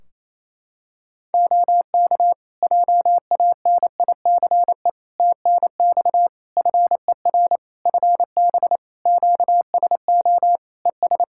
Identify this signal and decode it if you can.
Morse code, heard as a keyed tone, about 28 wpm, 700 Hz. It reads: OK JANICE TNX FER FB QSO ES
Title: K1USN SST QSO Practice 28wpm